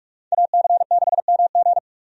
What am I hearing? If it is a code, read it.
Morse code, heard as a keyed tone, about 45 wpm, 700 Hz. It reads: AC6MG
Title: USA Call Signs - 5 Characters - 45wpm